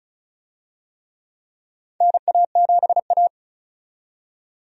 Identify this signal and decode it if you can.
Morse code, heard as a keyed tone, about 34 wpm, 700 Hz. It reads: NA7A
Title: USA Call Signs - 4 Characters - 35wpm